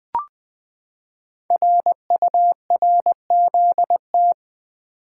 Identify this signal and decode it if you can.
Morse code, heard as a keyed tone, about 20 wpm, 700 Hz. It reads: RURZT
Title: Code Groups - 5 Letters - 20wpm